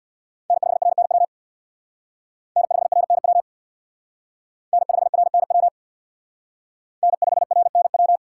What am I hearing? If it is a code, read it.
Morse code, heard as a keyed tone, about 62 wpm, 700 Hz. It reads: N5RNW N5RNW N5RNW N5RNW
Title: Speed Racing - USA Call Signs 50wpm